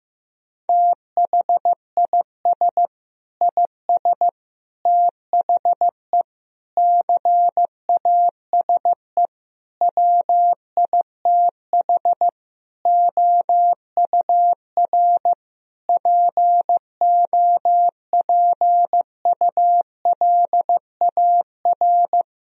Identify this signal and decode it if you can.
Morse code, heard as a keyed tone, about 15 wpm, 700 Hz. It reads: THIS IS THE CASE WITH OUR POPULAR
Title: Sentences from Top 1000 Words - Review Part 3 - 15wpm